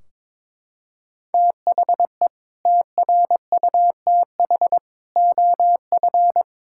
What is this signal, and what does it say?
Morse code, heard as a keyed tone, about 22 wpm, 700 Hz. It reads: THE TRUTH OF
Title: Sentences from Top 1600 Words 22wpm